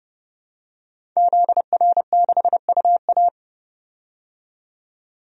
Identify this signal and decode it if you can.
Morse code, heard as a keyed tone, about 32 wpm, 690 Hz. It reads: ZR6UA
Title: DXCC-100 Call Signs - 5 Characters - 30wpm